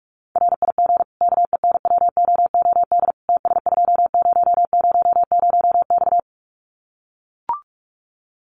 Morse code, heard as a keyed tone, about 44 wpm, 700 Hz. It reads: RIG KENWOOD TS2000X
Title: Common QSO Phrases 45wpm